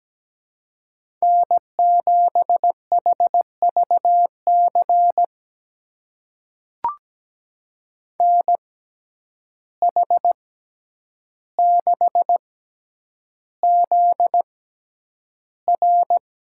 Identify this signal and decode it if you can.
Morse code, heard as a keyed tone, about 17 wpm, spaced as for 9 wpm, 700 Hz. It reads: N7HVC NH6ZR
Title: USA Call Signs - 5 Characters - 6x Character Spacing - 17wpm